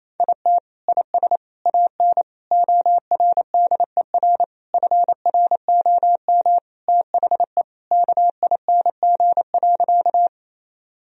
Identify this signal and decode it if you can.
Morse code, heard as a keyed tone, about 28 wpm, 700 Hz. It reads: IT IS AN ORDER FROM THE KING.
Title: Sentences from Top 1000 Words - Review Part 2 - 28wpm